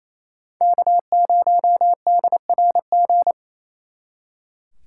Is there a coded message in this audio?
Yes: Morse code from 0.6 to 3.3 s, heard as a keyed tone, about 36 wpm, 700 Hz. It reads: K0DRG